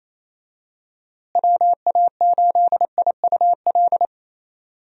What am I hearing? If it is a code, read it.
Morse code, heard as a keyed tone, about 28 wpm, 700 Hz. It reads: WA8IUL
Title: USA Call Signs - 6 Characters - Repeated 3x - 28wpm